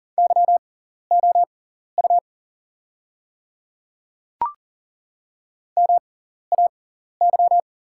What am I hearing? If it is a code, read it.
Morse code, heard as a keyed tone, about 40 wpm, spaced as for 14 wpm, 700 Hz. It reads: YOU MAY
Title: 6x Character Spacing - Top 500 Words - 40wpm